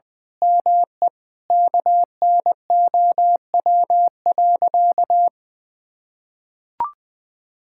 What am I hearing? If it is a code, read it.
Morse code, heard as a keyed tone, about 20 wpm, 700 Hz. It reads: ME KNOW.
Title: Sentences with 8-Word Phrases - 20wpm